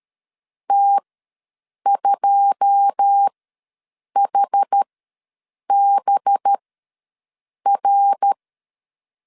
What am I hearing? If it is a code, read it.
Morse code, heard as a keyed tone, about 13 wpm, spaced as for 7 wpm, 800 Hz. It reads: T2HBR
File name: LEZIONI / 10 SIGNS / 10D COMA ws 12wpm 3618